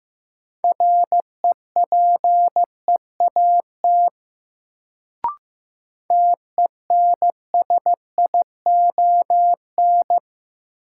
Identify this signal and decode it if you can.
Morse code, heard as a keyed tone, about 15 wpm, 700 Hz. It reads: REPEAT TENSION
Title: Sentences from Top 1600 Words - New Word Drill - 15wpm